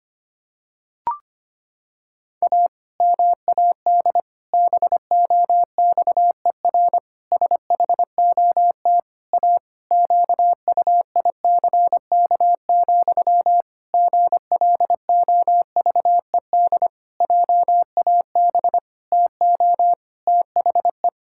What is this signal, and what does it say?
Morse code, heard as a keyed tone, about 25 wpm, 700 Hz. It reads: A MAD BOXER SHOT A QUICK, GLOVED JAB TO THE
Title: Pangram Challenge v2 - 25wpm